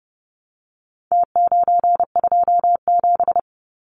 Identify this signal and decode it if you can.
Morse code, heard as a keyed tone, about 30 wpm, 700 Hz. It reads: T927
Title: Numbers - Contest Style - 4-Digit 30wpm